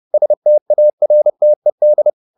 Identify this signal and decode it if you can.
Morse code, heard as a keyed tone, about 30 wpm, 600 Hz. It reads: STARTED